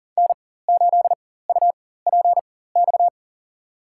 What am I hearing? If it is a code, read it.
Morse code, heard as a keyed tone, about 40 wpm, spaced as for 19 wpm, 700 Hz. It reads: N8UPX
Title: USA Call Signs - 4x Character Spacing - 40wpm